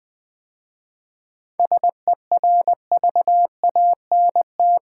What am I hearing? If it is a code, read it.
Morse code, heard as a keyed tone, about 20 wpm, 700 Hz. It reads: SERVANT
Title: 7 Letter Words - First 3 Letters Given - 20wpm